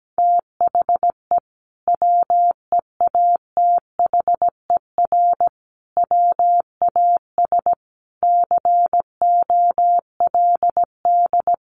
Morse code, heard as a keyed tone, about 17 wpm, 700 Hz. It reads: THE WEATHER WAS COLD
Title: Sentences from Top 1500 Words 17wpm